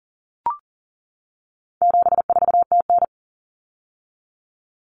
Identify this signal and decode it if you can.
Morse code, heard as a keyed tone, about 37 wpm, 700 Hz. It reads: 74TN
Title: Common Cut Numbers - 4-Digit 40wpm